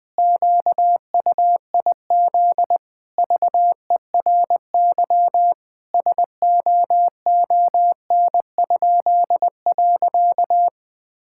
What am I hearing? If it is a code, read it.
Morse code, heard as a keyed tone, about 20 wpm, 700 Hz. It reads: QUIZ VERY SOON?.
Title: Pangram Challenge v2 - 20wpm